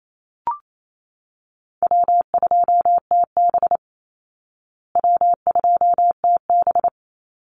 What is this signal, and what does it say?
Morse code, heard as a keyed tone, about 28 wpm, 700 Hz. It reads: W2TB W2TB
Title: Call Signs Repeated 2x - 28wpm